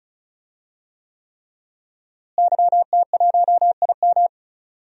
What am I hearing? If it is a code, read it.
Morse code, heard as a keyed tone, about 34 wpm, 700 Hz. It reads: YT1IM
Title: DXCC-100 Call Signs - 5 Characters - Repeated 3x - 35wpm